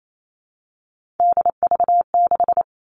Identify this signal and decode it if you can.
Morse code, heard as a keyed tone, about 28 wpm, 700 Hz. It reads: DV6
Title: All Cut Numbers - 3-Digit 28wpm